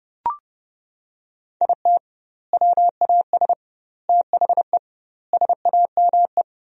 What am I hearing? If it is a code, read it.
Morse code, heard as a keyed tone, about 30 wpm, 700 Hz. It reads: IT WAS THE SAME
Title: 2x Word-Spacing - Sentences from Top 500 Words - 30wpm